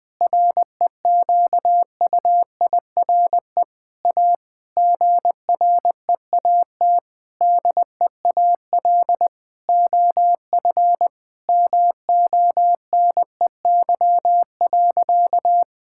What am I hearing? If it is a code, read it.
Morse code, heard as a keyed tone, about 20 wpm, 700 Hz. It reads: REQUIRE A GREAT DEAL OF MONEY.
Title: Sentences from Top 1000 Words - Review Part 6 - 20wpm